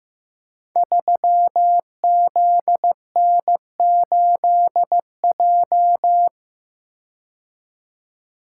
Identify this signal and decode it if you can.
Morse code, heard as a keyed tone, about 15 wpm, 700 Hz. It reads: 3ZN8J
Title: Code Groups - 5 Letter:Numbers - 15wpm